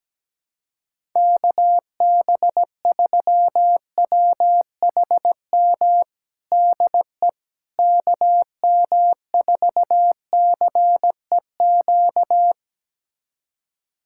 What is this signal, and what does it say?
Morse code, heard as a keyed tone, about 17 wpm, 700 Hz. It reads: KB3WHM DE KM4CEQ
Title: USA Callsign DE Callsign - 6 Characters - 17wpm